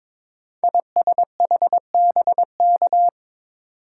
Morse code, heard as a keyed tone, about 22 wpm, 700 Hz. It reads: ISHBK
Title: Code Groups - 5 Letters - 22wpm